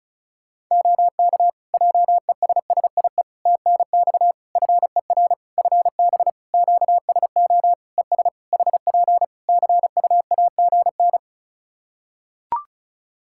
Morse code, heard as a keyed tone, about 36 wpm, 700 Hz. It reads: OK JESSIE TNX FER FB QSO ES HP CUAGN
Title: Practice QSO - Exchange 4 - Ending - 35wpm